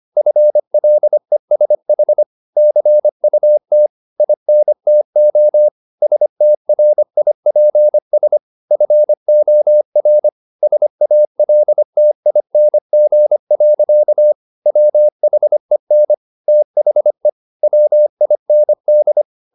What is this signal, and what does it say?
Morse code, heard as a keyed tone, about 25 wpm, 600 Hz. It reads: FLESH CUT INTO STRIPS FOR SALTING. WHEN THE WIND
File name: the old man and the sea part 1 25wpm